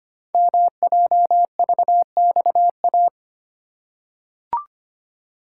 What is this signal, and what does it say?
Morse code, heard as a keyed tone, about 25 wpm, 700 Hz. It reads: MJVXA
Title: Code Groups - 5 Letters - 25wpm